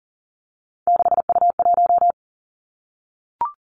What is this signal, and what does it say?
Morse code, heard as a keyed tone, about 38 wpm, 690 Hz. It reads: 6U1
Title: Three Challenging Letter-Number Combinations 40wpm